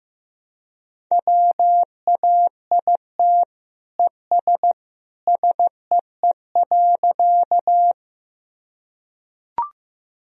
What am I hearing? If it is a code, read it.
Morse code, heard as a keyed tone, about 15 wpm, 700 Hz. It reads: WAIT ES SEE.
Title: Sentences from Top 500 Words and Abbreviations 15wpm